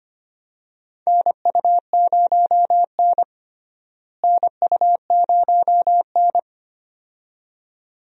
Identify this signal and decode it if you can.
Morse code, heard as a keyed tone, about 25 wpm, 700 Hz. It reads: NU0N NU0N
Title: USA Call Signs - 4 Characters - Repeated 2x - 25wpm